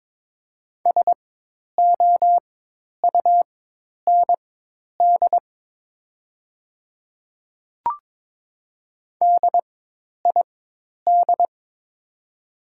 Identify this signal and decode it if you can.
Morse code, heard as a keyed tone, about 21 wpm, spaced as for 10 wpm, 700 Hz. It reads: SOUND DID
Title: 4x Character Spacing - Top 100 Words - 22wpm